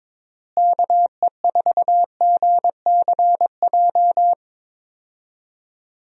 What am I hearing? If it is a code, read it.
Morse code, heard as a keyed tone, about 22 wpm, 700 Hz. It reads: KE4GCJ
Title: USA Call Signs - 6 Characters - Repeated 3x - 22wpm